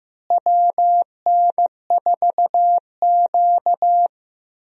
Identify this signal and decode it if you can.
Morse code, heard as a keyed tone, about 15 wpm, 700 Hz. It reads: WN4Q
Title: USA Call Signs - 4 Characters - Repeated 3x - 15wpm